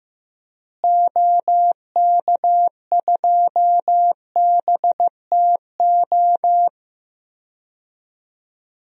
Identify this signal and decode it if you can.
Morse code, heard as a keyed tone, about 15 wpm, 700 Hz. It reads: OK2BTO
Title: DXCC-100 Call Signs - 6 Characters - Repeated 2x - 15wpm